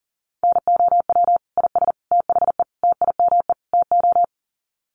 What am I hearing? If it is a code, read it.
Morse code, heard as a keyed tone, about 40 wpm, 700 Hz. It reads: NOW IS THE TIME TO
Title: Sets of 5 Words Encore2 40wpm